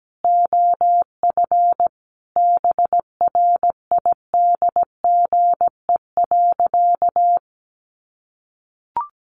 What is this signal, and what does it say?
Morse code, heard as a keyed tone, about 17 wpm, 700 Hz. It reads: OF BRIDGE.
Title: Sentences from Top 1200 Words 17wpm